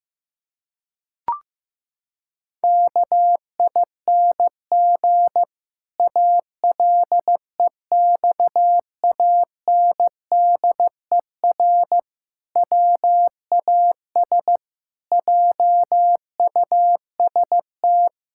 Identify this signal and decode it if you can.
Morse code, heard as a keyed tone, about 15 wpm, 700 Hz. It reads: KING ALEXANDER WAS JUST
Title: Pangram Challenge v2 - 15wpm